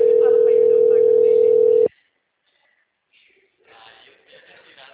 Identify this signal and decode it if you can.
Ringback tone (North America), 440 + 480 Hz.